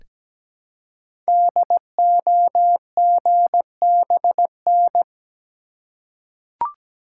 Morse code, heard as a keyed tone, about 17 wpm, 700 Hz. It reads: DOGBN
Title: Code Groups - 5 Letters - 17wpm